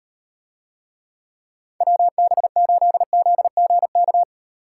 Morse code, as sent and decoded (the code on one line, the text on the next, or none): .-- -... ---.. --.. --. -.-
WB8ZGK